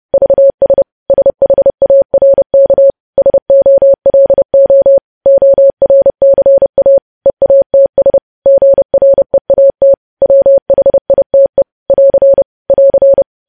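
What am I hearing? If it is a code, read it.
Morse code, heard as a keyed tone, about 30 wpm, 550 Hz. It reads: VS SHARK SOLO ORCA EATS GREAT WHITE <AR> <AR>
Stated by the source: Morse code practice of news headlines at 30.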